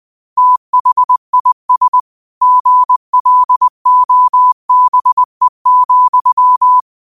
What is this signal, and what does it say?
Morse code, heard as a keyed tone, about 20 wpm, 1000 Hz. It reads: THIS GLOBE,
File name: qotdfiles / 20wpm